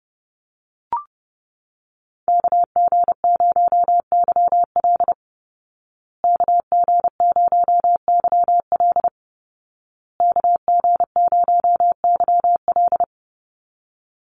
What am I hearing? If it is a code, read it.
Morse code, heard as a keyed tone, about 30 wpm, 700 Hz. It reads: KG0YL KG0YL KG0YL
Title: Call Signs Repeated 2x - 30wpm